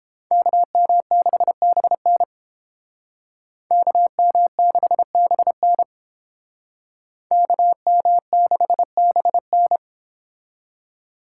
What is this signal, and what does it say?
Morse code, heard as a keyed tone, about 31 wpm, 700 Hz. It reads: KM6BN KM6BN KM6BN